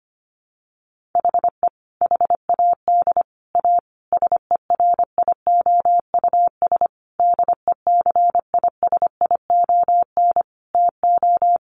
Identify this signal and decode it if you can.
Morse code, heard as a keyed tone, about 25 wpm, 700 Hz. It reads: HE HAD A SERIOUS DECISION TO
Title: Sentences from Top 1100 Words 25wpm